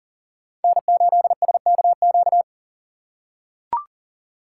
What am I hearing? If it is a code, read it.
Morse code, heard as a keyed tone, about 42 wpm, 700 Hz. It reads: N8SKQ
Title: USA Call Signs - 6x Character Spacing - 40wpm